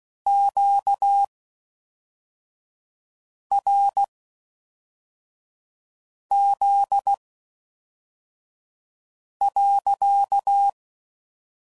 Morse code, heard as a keyed tone, about 16 wpm, spaced as for 4 wpm, 800 Hz. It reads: QRZ.